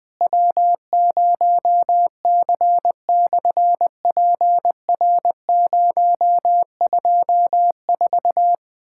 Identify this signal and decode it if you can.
Morse code, heard as a keyed tone, about 20 wpm, 700 Hz. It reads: W0C/PR024
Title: SOTA QSO Practice 20wpm